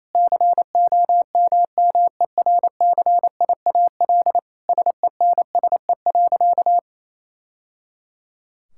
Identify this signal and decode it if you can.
Morse code, heard as a keyed tone, about 28 wpm, 700 Hz. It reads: COMMERCIAL SENSE.
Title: Sentences from Top 1000 Words - Review Part 5 - 28wpm